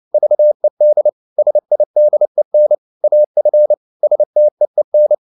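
Morse code, heard as a keyed tone, about 29 wpm, 600 Hz. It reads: VED SIDEN AF STEEN
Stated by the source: Fast (QRQ) Morse at 29 wpm, from CW med Gnister.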